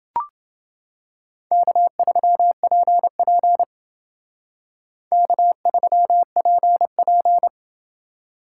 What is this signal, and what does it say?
Morse code, heard as a keyed tone, about 28 wpm, 700 Hz. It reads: K3PP K3PP